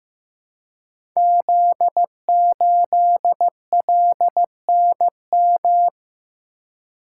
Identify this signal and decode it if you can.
Morse code, heard as a keyed tone, about 15 wpm, 700 Hz. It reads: Z8LNM
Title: Code Groups - 5 Letter:Numbers - 15wpm